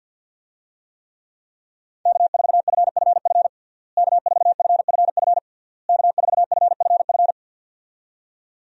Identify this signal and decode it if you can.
Morse code, heard as a keyed tone, about 52 wpm, 700 Hz. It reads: K4FFF K4FFF K4FFF
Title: USA Call Signs - 5 Characters - Repeated 3x - 50wpm